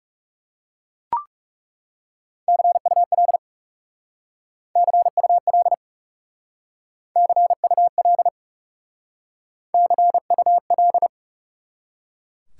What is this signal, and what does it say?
Morse code, heard as a keyed tone, about 38 wpm, 700 Hz. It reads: CUL CUL CUL CUL